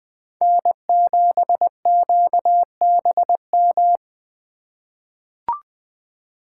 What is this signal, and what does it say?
Morse code, heard as a keyed tone, about 20 wpm, 700 Hz. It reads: N7QBM
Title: USA Call Signs - 5 Characters - 4x Character Spacing - 20wpm